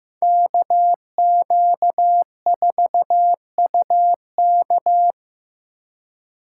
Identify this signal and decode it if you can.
Morse code, heard as a keyed tone, about 18 wpm, 700 Hz. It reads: KQ4UK
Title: Speed Racing - USA 2x2 Call Signs 15wpm